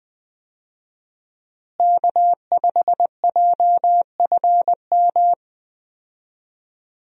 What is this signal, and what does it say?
Morse code, heard as a keyed tone, about 21 wpm, 700 Hz. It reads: K5JFM
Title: USA Call Signs - 5 Characters - Repeated 3x - 20wpm